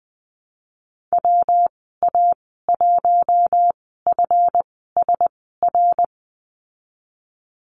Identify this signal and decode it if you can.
Morse code, heard as a keyed tone, about 20 wpm, spaced as for 15 wpm, 700 Hz. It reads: WA1FSR
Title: 147 Lesson 24g - Letter y - Call Signs 20wpm